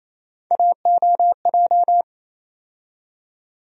Morse code, heard as a keyed tone, about 27 wpm, 700 Hz. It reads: AOJ
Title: USA Call Sign Postfixes - 3 Letters 28wpm